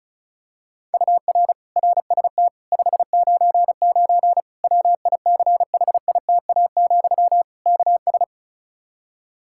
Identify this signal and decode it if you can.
Morse code, heard as a keyed tone, about 36 wpm, 700 Hz. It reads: UR RST 599 WICHITA, KS
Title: Practice QSO - Exchange 1 - Beginning - 35wpm